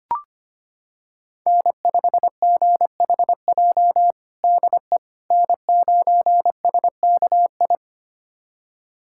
Morse code, heard as a keyed tone, about 25 wpm, 700 Hz. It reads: N5GHJ DE N9SKI
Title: USA Callsign DE Callsign - 5 Characters - 25wpm